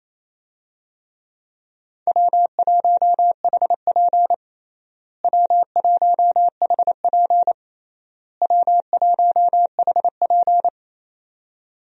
Morse code, heard as a keyed tone, about 28 wpm, 700 Hz. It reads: W1HP W1HP W1HP
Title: USA Call Signs - 4 Characters - Repeated 3x - 28wpm